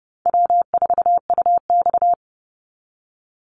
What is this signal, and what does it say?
Morse code, heard as a keyed tone, about 30 wpm, 700 Hz. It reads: W4UX
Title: Easy Call Signs Repeated 3x - 30wpm